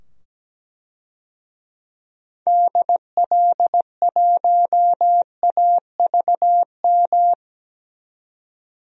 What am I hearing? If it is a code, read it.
Morse code, heard as a keyed tone, about 17 wpm, 700 Hz. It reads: DL1AVM
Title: DXCC-100 Call Signs - 6 Characters - 17wpm